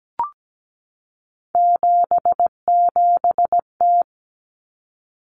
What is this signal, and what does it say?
Morse code, heard as a keyed tone, about 17 wpm, 690 Hz. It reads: 77T
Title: Common Cut Numbers - 3-Digit 17wpm